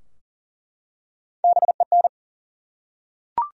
Morse code, heard as a keyed tone, about 40 wpm, 700 Hz. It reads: BEN
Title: USA Common Names - Max 3 Letters - 40wpm